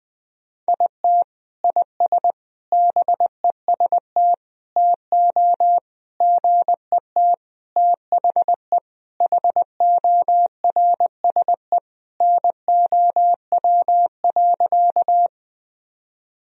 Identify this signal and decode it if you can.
Morse code, heard as a keyed tone, about 20 wpm, 700 Hz. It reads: IT IS BEST TO GET THE HORSE NOW.
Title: Sentences from Top 1000 Words - Review Part 1 - 20wpm